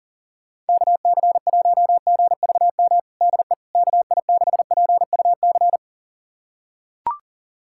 Morse code, heard as a keyed tone, about 37 wpm, 700 Hz. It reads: KC1GVM DE KI6PUC
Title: USA Callsign DE Callsign 40wpm